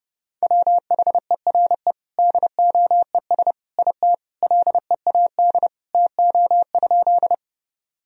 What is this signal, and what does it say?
Morse code, heard as a keyed tone, about 30 wpm, 700 Hz. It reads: WHERE DOES IT LEAD TO?
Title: Sentences from Top 1000 Words - Review Part 2 - 30wpm